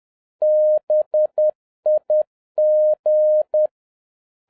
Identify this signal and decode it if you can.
Morse code, heard as a keyed tone, about 10 wpm, 610 Hz. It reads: BIG
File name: 2011-07-mp3 / n41 EXCITING QRP CW OPERATION 10wpm